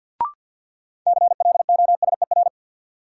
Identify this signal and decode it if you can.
Morse code, heard as a keyed tone, about 46 wpm, 700 Hz. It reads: CLOSER
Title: Sets of 5 Words Encore - New Word Drill - 50wpm